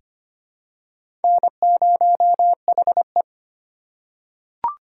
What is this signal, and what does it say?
Morse code, heard as a keyed tone, about 25 wpm, 700 Hz. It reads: N0HE
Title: USA Call Signs - 4 Characters - 4x Character Spacing - 25wpm